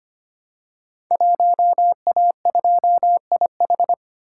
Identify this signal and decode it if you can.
Morse code, heard as a keyed tone, about 25 wpm, 700 Hz. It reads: 1A2IH
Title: Code Groups - 5 Letter:Numbers - 25wpm